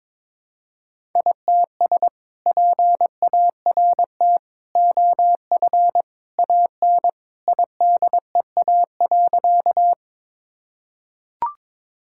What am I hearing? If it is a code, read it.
Morse code, heard as a keyed tone, about 22 wpm, 700 Hz. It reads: ITS PART OF AN IDEA.
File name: practice / 22wpm / Sentences with 2-Word Phrases - 22wpm